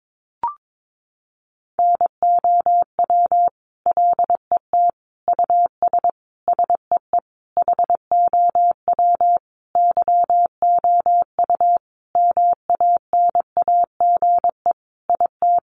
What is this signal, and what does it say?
Morse code, heard as a keyed tone, about 22 wpm, 700 Hz. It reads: NOW LET US SEE HOW YOU MANAGE IT
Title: Sentences from Top 1400 Words 22wpm